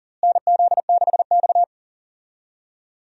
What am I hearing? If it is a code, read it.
Morse code, heard as a keyed tone, about 40 wpm, 700 Hz. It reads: NZ6X
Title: USA Call Signs - 4 Characters - Repeated 2x - 40wpm